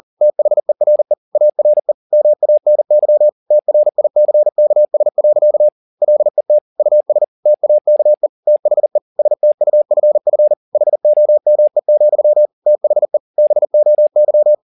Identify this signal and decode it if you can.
Morse code, heard as a keyed tone, about 40 wpm, 600 Hz. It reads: THERE ARE MANY TRICKS. LET US TAKE THE STUFF HOME, THE BOY
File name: the old man and the sea 1 40wpm